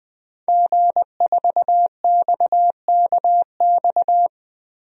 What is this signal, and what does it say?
Morse code, heard as a keyed tone, about 20 wpm, 700 Hz. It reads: G4XKX
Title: DXCC-100 Call Signs - 5 Characters - 20wpm